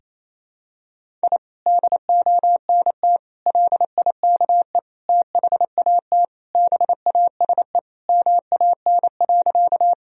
Morse code, heard as a keyed tone, about 28 wpm, 700 Hz. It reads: I DONT LIKE THAT BASE MAN.
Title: Sentences from Top 1000 Words - Review Part 1 - 28wpm